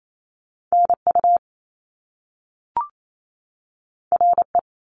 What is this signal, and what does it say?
Morse code, heard as a keyed tone, about 28 wpm, 700 Hz. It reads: N U R E
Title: Top Two Letter Combinations in Words 28wpm